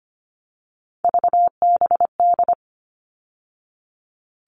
Morse code, heard as a keyed tone, about 25 wpm, 700 Hz. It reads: VBD
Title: All Cut Numbers - 3-Digit 25wpm